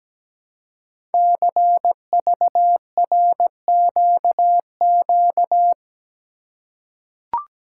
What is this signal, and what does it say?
Morse code, heard as a keyed tone, about 17 wpm, 700 Hz. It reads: CVRQQ
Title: Code Groups - 5 Letters - 17wpm